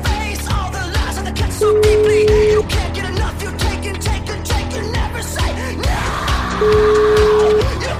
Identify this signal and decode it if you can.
Ringback tone (Europe, 425 Hz), with noise.